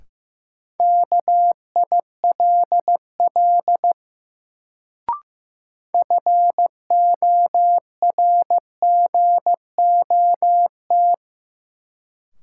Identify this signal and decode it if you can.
Morse code, heard as a keyed tone, about 15 wpm, 700 Hz. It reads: KILL FORGOT
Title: Sentences from Top 1400 Words - New Word Drill - 15wpm